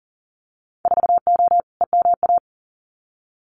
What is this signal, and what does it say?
Morse code, heard as a keyed tone, about 40 wpm, 700 Hz. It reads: 4O EMA
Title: Winter Field Day Exchanges Encore 40wpm